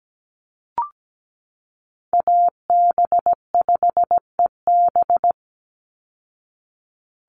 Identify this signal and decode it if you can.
Morse code, heard as a keyed tone, about 17 wpm, 700 Hz. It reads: AB5EB
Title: Call Signs Repeated 2x - 17wpm